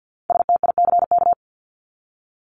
Morse code, heard as a keyed tone, about 49 wpm, 700 Hz. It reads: STICK
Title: Top 1001-1250 Words 50wpm